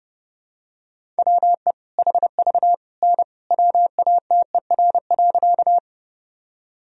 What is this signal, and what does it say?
Morse code, heard as a keyed tone, about 30 wpm, 700 Hz. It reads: WE HV N WATER.
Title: Sentences from Top 100 Words and Abbreviations 30wpm